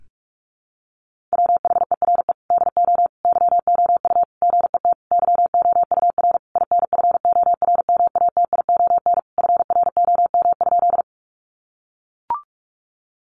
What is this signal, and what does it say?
Morse code, heard as a keyed tone, about 44 wpm, 700 Hz. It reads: WHERE DO YOU GET YOUR INFORMATION FROM?
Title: Sentences from Top 600 Words 45wpm